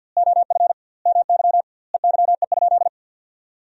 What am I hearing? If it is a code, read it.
Morse code, heard as a keyed tone, about 50 wpm, 700 Hz. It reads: OF MY EYE?
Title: 2x Word-Spacing - Sentences from Top 500 Words - 50wpm